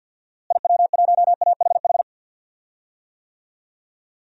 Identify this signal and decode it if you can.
Morse code, heard as a keyed tone, about 47 wpm, 700 Hz. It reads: IW1AHH
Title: DXCC-100 Call Signs - 6 Characters - Repeated 2x - 50wpm